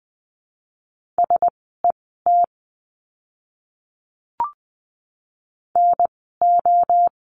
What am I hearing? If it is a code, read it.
Morse code, heard as a keyed tone, about 20 wpm, 700 Hz. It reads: S E T N O
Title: 036 Lesson 7f - Letter s - Words 20wpm